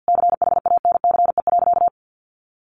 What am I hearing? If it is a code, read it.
Morse code, heard as a keyed tone, about 50 wpm, 700 Hz. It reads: CHANCE.
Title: Sentences from Top 1400 Words 50wpm